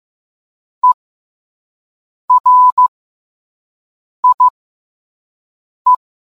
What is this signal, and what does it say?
Morse code, heard as a keyed tone, about 15 wpm, spaced as for 5 wpm, 1000 Hz. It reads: ERIE